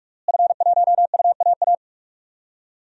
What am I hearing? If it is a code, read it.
Morse code, heard as a keyed tone, about 42 wpm, 700 Hz. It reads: F1UAA
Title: DXCC-100 Call Signs - 5 Characters - Repeated 3x - 45wpm